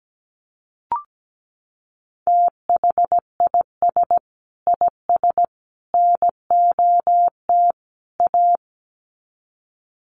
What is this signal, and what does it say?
Morse code, heard as a keyed tone, about 17 wpm, 700 Hz. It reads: THIS IS NOT A
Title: Sets of 4 Words 17wpm